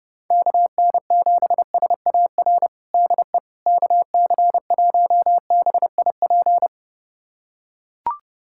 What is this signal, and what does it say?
Morse code, heard as a keyed tone, about 30 wpm, 700 Hz. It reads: KN7SAR DE KC1BIP
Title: USA Callsign DE Callsign 30wpm